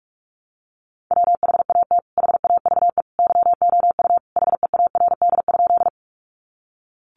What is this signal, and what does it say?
Morse code, heard as a keyed tone, about 44 wpm, 700 Hz. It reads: WHAT HAVE YOU HEARD?
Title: Sentences from Top 400 Words 45wpm